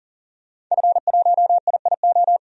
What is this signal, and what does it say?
Morse code, heard as a keyed tone, about 40 wpm, 700 Hz. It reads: F1IIO
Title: DXCC-100 Call Signs - 5 Characters - 40wpm